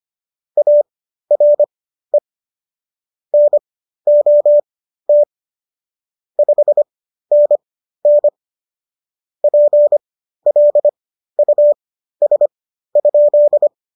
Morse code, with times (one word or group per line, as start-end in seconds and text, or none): 0.6-2.2 ARE
3.3-5.2 NOT
6.4-8.3 5NN
9.4-13.7 PLUS?